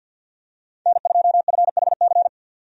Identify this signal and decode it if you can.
Morse code, heard as a keyed tone, about 50 wpm, 700 Hz. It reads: N2FHC
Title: USA Call Signs - 5 Characters - 50wpm